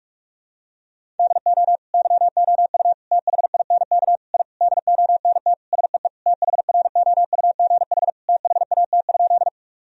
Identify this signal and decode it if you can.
Morse code, heard as a keyed tone, about 46 wpm, 700 Hz. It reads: DO YOU THINK I DONT SEE THROUGH THAT?
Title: Sentences from Top 1000 Words - Review Part 1 - 45wpm